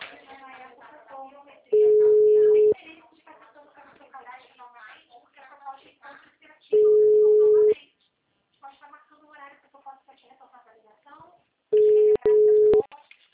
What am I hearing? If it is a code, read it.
Ringback tone (Europe, 425 Hz), with speech.
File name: out- -2018- - - .67747